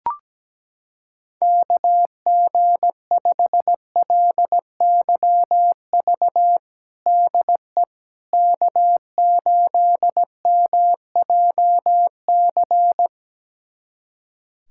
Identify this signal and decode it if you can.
Morse code, heard as a keyed tone, about 17 wpm, 700 Hz. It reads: KG5LYV DE K8MJC
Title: USA Callsign DE Callsign 17wpm